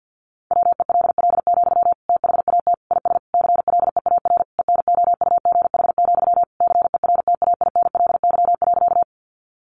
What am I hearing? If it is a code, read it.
Morse code, heard as a keyed tone, about 50 wpm, 700 Hz. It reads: WELL, THAT IS CLEAR ENOUGH, CERTAINLY.
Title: Sentences from Top 900 Words 50wpm